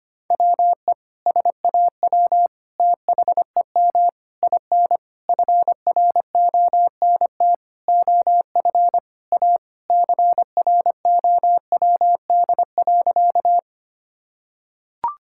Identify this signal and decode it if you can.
Morse code, heard as a keyed tone, about 25 wpm, 700 Hz. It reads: WE SAW THEM IN FRONT OF A CROWD.
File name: practice / 25wpm / Sentences with 4-Word Phrases - 25wpm